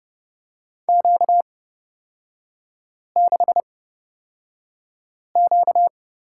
Morse code, heard as a keyed tone, about 30 wpm, spaced as for 5 wpm, 700 Hz. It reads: Q6Q
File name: Lesson 38e - Letter q - Three Characters 30wpm